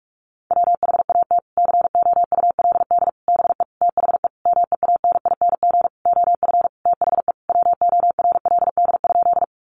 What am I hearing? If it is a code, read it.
Morse code, heard as a keyed tone, about 44 wpm, 700 Hz. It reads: WHAT COULD BE THE MEANING OF THE WORLD?
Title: Sentences from Top 900 Words 45wpm